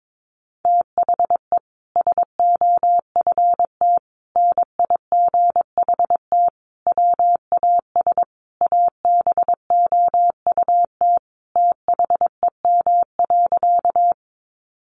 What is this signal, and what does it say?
Morse code, heard as a keyed tone, about 22 wpm, 700 Hz. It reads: THE SOFT NIGHT WAS ABOUT THEM.
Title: Sentences from Top 1500 Words 22wpm